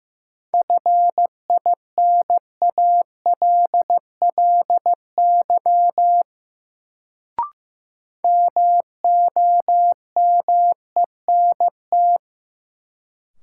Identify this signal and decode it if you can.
Morse code, heard as a keyed tone, about 15 wpm, 700 Hz. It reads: FINALLY MOMENT
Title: Sentences from Top 1000 Words - New Word Drill - 15wpm